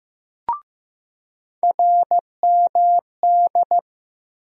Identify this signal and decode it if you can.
Morse code, heard as a keyed tone, about 15 wpm, 700 Hz. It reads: RMD
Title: USA Call Sign Postfixes - 3 Letters 15wpm